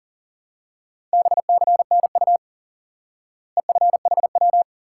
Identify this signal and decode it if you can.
Morse code, heard as a keyed tone, about 37 wpm, 700 Hz. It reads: BCNU EFHW
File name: IQE Course 3 - Lesson 89 - YAGI - Two Elements - 40wpm